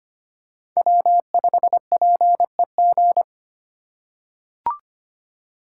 Morse code, heard as a keyed tone, about 25 wpm, 700 Hz. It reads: W5PEG
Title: USA Call Signs - 5 Characters - 2x Character Spacing - 25wpm